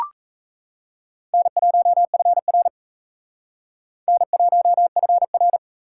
Morse code, heard as a keyed tone, about 34 wpm, 700 Hz. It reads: N1FR N1FR
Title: Speed Racing - USA 1x2 Call Signs 28wpm